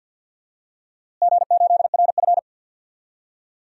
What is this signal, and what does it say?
Morse code, heard as a keyed tone, about 47 wpm, 700 Hz. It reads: G8RF
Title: DXCC-100 Call Signs - 4 Characters - Repeated 3x - 50wpm